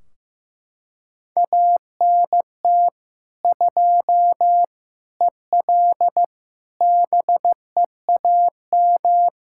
Morse code, heard as a keyed tone, about 15 wpm, 700 Hz. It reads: ANT 2 EL BEAM
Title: Practice QSO - Exchange 2 - Info - 15wpm